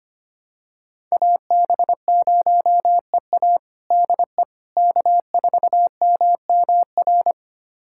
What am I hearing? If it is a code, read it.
Morse code, heard as a keyed tone, about 25 wpm, 700 Hz. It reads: AB0EA DE K4MMR
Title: USA Callsign DE Callsign - 5 Characters - 25wpm